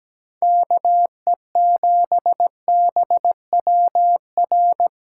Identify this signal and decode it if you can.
Morse code, heard as a keyed tone, about 17 wpm, 700 Hz. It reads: KE7BWR
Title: USA Callsign DE Callsign - 6 Characters - 17wpm